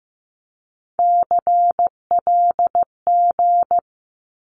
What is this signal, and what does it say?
Morse code, heard as a keyed tone, about 15 wpm, 700 Hz. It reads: CLG
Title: Abbreviations 15wpm